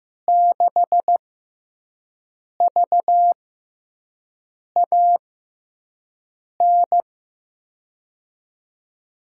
Morse code, heard as a keyed tone, about 15 wpm, spaced as for 5 wpm, 700 Hz. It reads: 6VAN
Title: USA Call Signs - 6 Characters - 6x Character Spacing - 15wpm